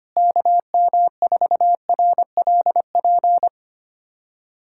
Morse code, heard as a keyed tone, about 25 wpm, 700 Hz. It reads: KM4RLP
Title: USA Callsign DE Callsign - 6 Characters - 25wpm